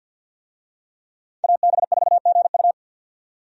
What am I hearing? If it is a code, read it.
Morse code, heard as a keyed tone, about 53 wpm, 700 Hz. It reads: AB4GU
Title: USA Call Signs - 5 Characters - 50wpm